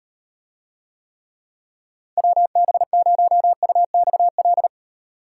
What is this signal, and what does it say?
Morse code, heard as a keyed tone, about 34 wpm, 700 Hz. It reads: WB0UXL